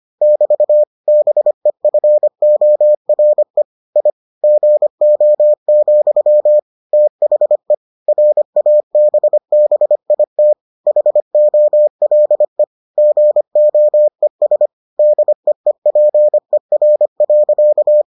Morse code, heard as a keyed tone, about 25 wpm, 600 Hz. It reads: <BT> BEFORE I GO, THE RABBIT HOLE GOES DEEPER.